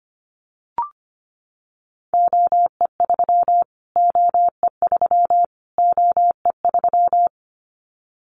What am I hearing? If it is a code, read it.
Morse code, heard as a keyed tone, about 25 wpm, 700 Hz. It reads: OE3 OE3 OE3
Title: Callsign Prefixes for Top 25 DXCC - Repeated 3x - 25wpm